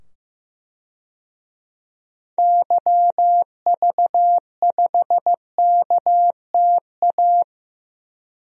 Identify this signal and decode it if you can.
Morse code, heard as a keyed tone, about 15 wpm, 700 Hz. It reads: YV5KTA
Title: DXCC-100 Call Signs - 6 Characters - 15wpm